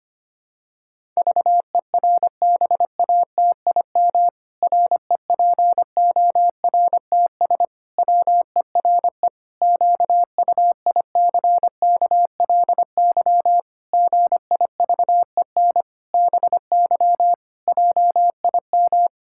Morse code, heard as a keyed tone, about 25 wpm, 700 Hz. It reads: VERBATIM REPORTS WERE QUICKLY GIVEN BY JIM
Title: Pangram Challenge v2 - 25wpm